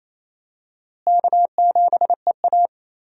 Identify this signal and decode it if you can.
Morse code, heard as a keyed tone, about 28 wpm, 700 Hz. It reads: K7EA
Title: USA Call Signs - 4 Characters - Repeated 3x - 28wpm